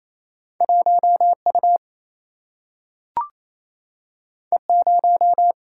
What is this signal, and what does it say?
Morse code, heard as a keyed tone, about 28 wpm, 700 Hz. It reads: 1 U E 0
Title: Single-Letter Single-Number 28wpm